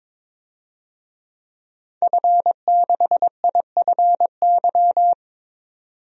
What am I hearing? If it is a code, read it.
Morse code, heard as a keyed tone, about 21 wpm, 700 Hz. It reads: F6IFY
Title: DXCC-100 Call Signs - 5 Characters - 22wpm